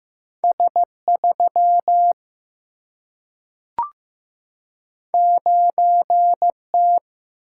Morse code, heard as a keyed tone, about 15 wpm, 700 Hz. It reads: S3 9T
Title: Single-Letter Single-Number 15wpm